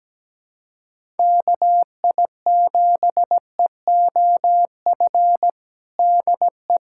Morse code, heard as a keyed tone, about 17 wpm, 700 Hz. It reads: KI7EOF DE
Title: USA Callsign DE Callsign - 6 Characters - 17wpm